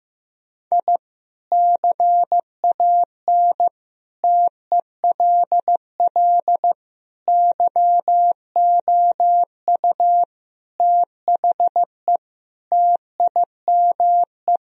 Morse code, heard as a keyed tone, about 15 wpm, 700 Hz. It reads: I CAN TELL YOU THE TIME
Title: Sentences from Top 1000 Words - Review Part 1 - 15wpm